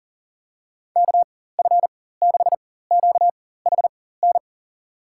Morse code, heard as a keyed tone, about 40 wpm, spaced as for 19 wpm, 700 Hz. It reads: KF6QHN
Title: USA Call Signs - 6 Characters - 4x Character Spacing - 40wpm